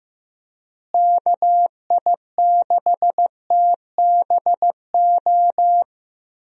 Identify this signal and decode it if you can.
Morse code, heard as a keyed tone, about 15 wpm, 700 Hz. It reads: KI6TBO
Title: USA Call Signs - 6 Characters - Repeated 3x - 15wpm